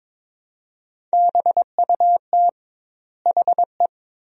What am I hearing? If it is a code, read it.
Morse code, heard as a keyed tone, about 22 wpm, 700 Hz. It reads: BUT HE
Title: 2x Word-Spacing - Sentences from Top 400 Words - 22wpm